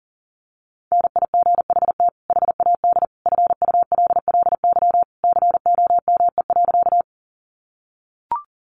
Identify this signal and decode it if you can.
Morse code, heard as a keyed tone, about 40 wpm, 700 Hz. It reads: NIGHT HAD FULLY COME.
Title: Sentences from Top 1300 Words 40wpm